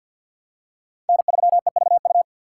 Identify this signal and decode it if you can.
Morse code, heard as a keyed tone, about 49 wpm, 700 Hz. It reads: N3EVU
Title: USA Call Signs - 5 Characters - 4x Character Spacing - 50wpm